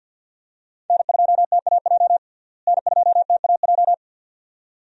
Morse code, heard as a keyed tone, about 51 wpm, 700 Hz. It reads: N2TAJ N2TAJ
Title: USA Call Signs - 5 Characters - Repeated 2x - 50wpm